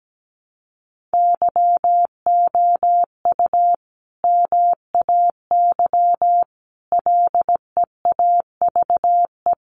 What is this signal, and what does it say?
Morse code, heard as a keyed tone, about 17 wpm, 700 Hz. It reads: YOU MAY LEAVE
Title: Sentences from Top 400 Words 17wpm